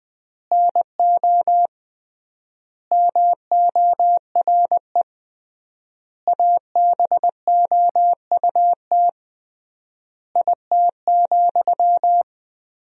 Morse code, heard as a keyed tone, about 20 wpm, 700 Hz. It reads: NO MORE ABOUT IT,
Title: 3x Word-Spacing - Sentences from Top 200 Words - 20wpm